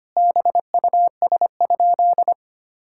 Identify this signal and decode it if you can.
Morse code, heard as a keyed tone, about 25 wpm, 700 Hz. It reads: BUS?
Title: Sentences from Top 1000 Words - Review Part 6 - 25wpm